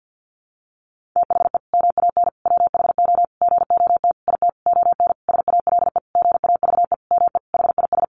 Morse code, heard as a keyed tone, about 50 wpm, 700 Hz. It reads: THE MAN WHO GOT IT ON SALE GAVE ME HIS
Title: Sentences from Top 1400 Words 50wpm